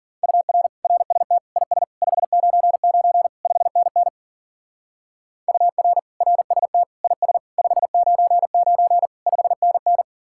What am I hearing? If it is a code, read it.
Morse code, heard as a keyed tone, about 47 wpm, 700 Hz. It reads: UR RST IS 599 5NN UR RST IS 599 5NN
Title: Speed-Racing Common QSO Phrases - 40wpm